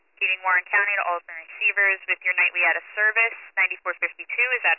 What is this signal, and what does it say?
Emergency Alert System (SAME) tones, heard as harsh buzzing bursts, with speech. It is heard from 0.4 to 3.7 s.